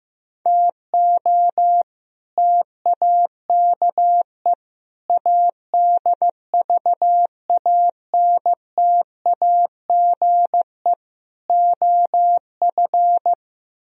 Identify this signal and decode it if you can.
Morse code, heard as a keyed tone, about 15 wpm, 700 Hz. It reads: TO TAKE ADVANTAGE OF
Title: Speed-Racing - Sets of 4 Words 15wpm